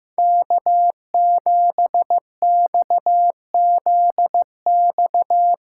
Morse code, heard as a keyed tone, about 15 wpm, 700 Hz. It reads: K7XZX
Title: USA Call Signs - 5 Characters - 6x Character Spacing - 15wpm